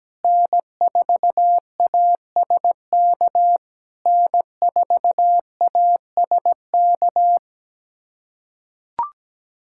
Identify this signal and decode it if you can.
Morse code, heard as a keyed tone, about 18 wpm, 700 Hz. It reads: N4ASK N4ASK
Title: SOTA QSO Practice 17wpm